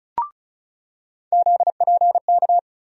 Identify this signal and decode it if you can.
Morse code, heard as a keyed tone, about 36 wpm, 700 Hz. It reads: ZPK
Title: USA Call Sign Postfixes - 3 Letters 35wpm